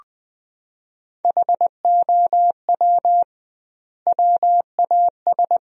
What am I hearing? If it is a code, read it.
Morse code, heard as a keyed tone, about 20 wpm, 700 Hz. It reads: HOW WAS
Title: 2x Word-Spacing - Sentences from Top 500 Words - 20wpm